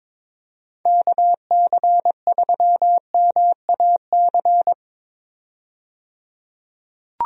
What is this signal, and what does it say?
Morse code, heard as a keyed tone, about 22 wpm, 700 Hz. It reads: KC3MAC
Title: USA Call Signs - 6 Characters - 22wpm